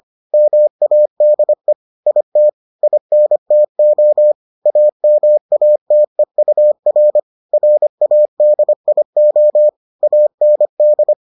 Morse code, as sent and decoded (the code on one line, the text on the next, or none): -- .- -.. / .. - / .. -. - --- / .- -- .- - . ..- .-. / .-. .- -.. .. --- / .- -. -..
MADE IT INTO AMATEUR RADIO AND